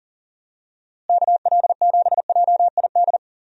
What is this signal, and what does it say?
Morse code, heard as a keyed tone, about 38 wpm, 700 Hz. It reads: KL7JID
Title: DXCC-100 Call Signs - 6 Characters - Repeated 2x - 40wpm